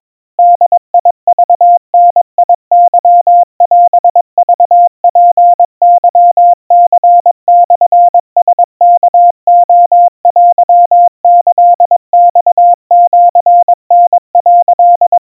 Morse code (Japanese Wabun code), heard as a keyed tone, about 22 wpm, 700 Hz. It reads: ボクダケオクツケニモラワレテキマシタ」
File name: kamyu 80 cpm 700 Hz Crispy